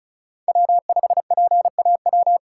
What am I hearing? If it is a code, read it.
Morse code, heard as a keyed tone, about 34 wpm, 700 Hz. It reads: W5PAW
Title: USA Call Signs - 8x Character Spacing - 35wpm